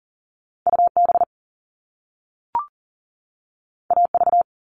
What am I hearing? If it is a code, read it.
Morse code, heard as a keyed tone, about 42 wpm, 700 Hz. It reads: UB AV
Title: All Cut Numbers - 2-Digit 40wpm